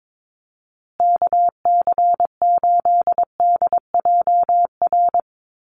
Morse code, heard as a keyed tone, about 22 wpm, 700 Hz. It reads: KC8DJR
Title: USA Call Signs Repeated 2x 22wpm